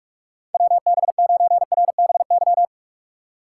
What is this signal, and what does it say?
Morse code, heard as a keyed tone, about 46 wpm, 700 Hz. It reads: WB9RBY
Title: USA Call Signs - 3x Character Spacing - 45wpm